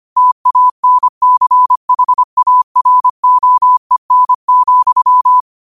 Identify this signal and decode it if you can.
Morse code, heard as a keyed tone, about 25 wpm, 1000 Hz. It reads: TANCHAROEN,
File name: qotdfiles / 25wpm